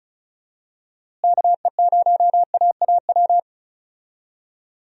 Morse code, heard as a keyed tone, about 36 wpm, 700 Hz. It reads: KE0AAW
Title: USA Call Signs - 6 Characters - Repeated 2x - 35wpm